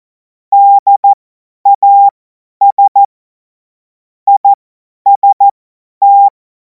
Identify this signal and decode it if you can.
Morse code, heard as a keyed tone, about 14 wpm, spaced as for 10 wpm, 800 Hz. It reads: DAS IST
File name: Morsekurs / Witze / Witz 10. März 2026 50 F70